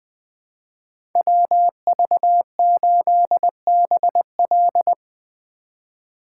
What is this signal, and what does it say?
Morse code, heard as a keyed tone, about 20 wpm, 700 Hz. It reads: WV8BL
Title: USA Call Signs - 4x Character Spacing - 20wpm